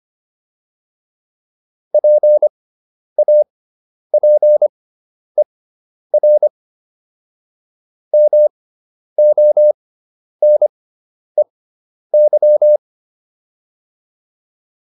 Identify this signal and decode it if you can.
Morse code, heard as a keyed tone, about 25 wpm, spaced as for 10 wpm, 600 Hz. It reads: PAPER MONEY